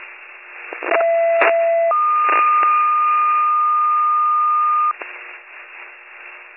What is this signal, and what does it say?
Two-tone page, heard as a one-second tone, then a three-second tone, about 670 then 1220 Hz. It is heard from 0.9 to 4.9 s.